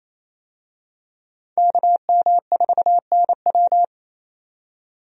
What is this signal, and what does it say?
Morse code, heard as a keyed tone, about 28 wpm, 700 Hz. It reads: KM4NW
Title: USA Call Signs - 5 Characters - 28wpm